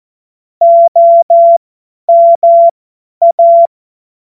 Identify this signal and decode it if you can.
Morse code, heard as a keyed tone, about 14 wpm, spaced as for 10 wpm, 680 Hz. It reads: OMA